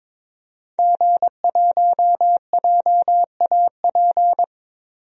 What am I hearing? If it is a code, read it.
Morse code, heard as a keyed tone, about 22 wpm, 700 Hz. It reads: G1JAP
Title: DXCC-100 Call Signs - 5 Characters - 22wpm